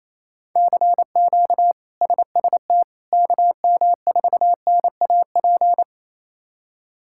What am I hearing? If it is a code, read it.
Morse code, heard as a keyed tone, about 28 wpm, 700 Hz. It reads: CQ SST KM4NAP
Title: K1USN SST QSO Practice - 28wpm